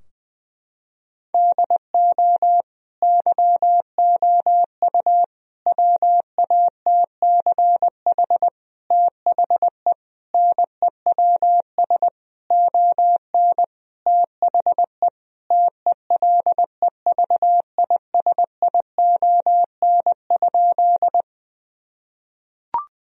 Morse code, heard as a keyed tone, about 20 wpm, 700 Hz. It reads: DO YOU WATCH THE NEWS ON THE TELEVISION?
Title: Sentences from Top 1000 Words - Review Part 4 - 20wpm